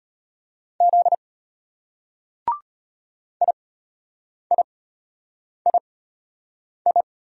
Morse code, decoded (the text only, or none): Z IIII